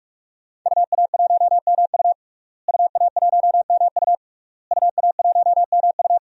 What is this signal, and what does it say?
Morse code, heard as a keyed tone, about 41 wpm, 700 Hz. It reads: UA1MU UA1MU UA1MU
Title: DXCC-100 Call Signs - 5 Characters - Repeated 3x - 45wpm